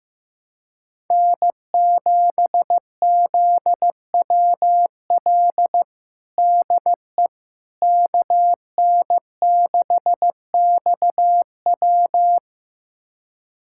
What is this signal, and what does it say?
Morse code, heard as a keyed tone, about 15 wpm, 700 Hz. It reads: N7ZWL DE KN6XW
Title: USA Callsign DE Callsign - 5 Characters - 15wpm